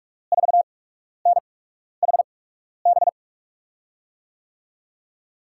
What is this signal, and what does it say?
Morse code, heard as a keyed tone, about 47 wpm, spaced as for 12 wpm, 700 Hz. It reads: 4NHB
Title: USA Call Signs - 8x Character Spacing - 45wpm